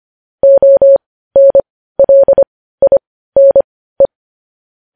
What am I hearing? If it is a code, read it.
Morse code, heard as a keyed tone, about 25 wpm, spaced as for 15 wpm, 550 Hz. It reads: ONLINE
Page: news-headlines-in-morse-code-at-15-wpm